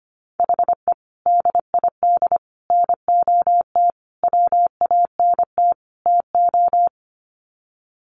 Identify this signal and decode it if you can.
Morse code, heard as a keyed tone, about 25 wpm, 700 Hz. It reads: HE DID NOT WANT TO
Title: Sets of 5 Words 25wpm